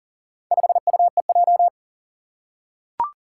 Morse code, heard as a keyed tone, about 40 wpm, 700 Hz. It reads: 5UEJ
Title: Four Challenging Letter-Number Combinations 40wpm